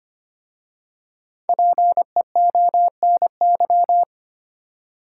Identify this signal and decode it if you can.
Morse code, heard as a keyed tone, about 25 wpm, 700 Hz. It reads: PEONY
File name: practice / 25wpm / Difficult Words with Hints - 25wpm